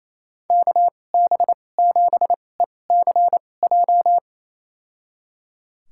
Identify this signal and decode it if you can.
Morse code, heard as a keyed tone, about 28 wpm, spaced as for 20 wpm, 690 Hz. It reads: KB7ECJ
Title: USA Call Signs - 2x Character Spacing - 28wpm